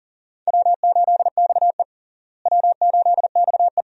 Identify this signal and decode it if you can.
Morse code, heard as a keyed tone, about 40 wpm, 700 Hz. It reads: W8XE W8XE
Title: USA Call Signs - 4 Characters - Repeated 2x - 40wpm